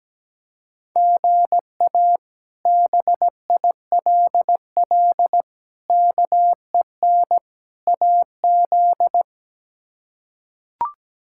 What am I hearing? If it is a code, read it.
Morse code, heard as a keyed tone, about 17 wpm, 700 Hz. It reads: GA BILL KEN AZ
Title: K1USN SST QSO Practice - 17wpm